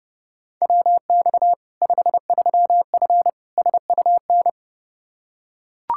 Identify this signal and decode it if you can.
Morse code, heard as a keyed tone, about 30 wpm, 700 Hz. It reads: WX 53F SUN
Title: Practice QSO - Exchange 2 - Info - 30wpm